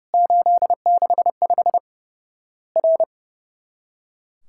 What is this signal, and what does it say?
Morse code, heard as a keyed tone, about 30 wpm, 700 Hz. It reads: 865 R
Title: Worked All Europe DX Contest - QTC Exchange - 30wpm